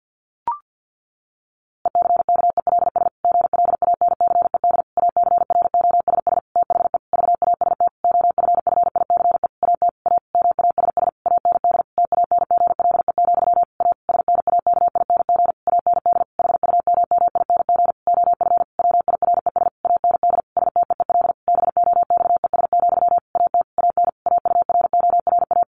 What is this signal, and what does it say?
Morse code, heard as a keyed tone, about 50 wpm, 700 Hz. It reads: ECKELS GLANCED ACROSS THE VAST OFFICE AT A MASS AND TANGLE, A SNAKING AND HUMMING OF WIRES AND STEEL BOXES, AT AN AURORA
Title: A Sound Of Thunder 50wpm